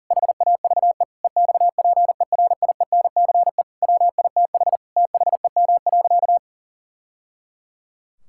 Morse code, heard as a keyed tone, about 37 wpm, 700 Hz. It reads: HAVE EXPERIENCE WITH THEM.
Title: Sentences from Top 1000 Words - Review Part 6 - 40wpm